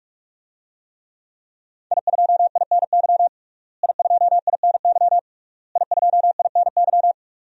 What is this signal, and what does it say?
Morse code, heard as a keyed tone, about 46 wpm, 700 Hz. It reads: I2INY I2INY I2INY
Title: DXCC-100 Call Signs - 5 Characters - Repeated 3x - 45wpm